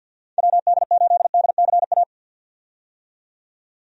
Morse code, heard as a keyed tone, about 50 wpm, 700 Hz. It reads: WD8DCA
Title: USA Call Signs - 6 Characters - 50wpm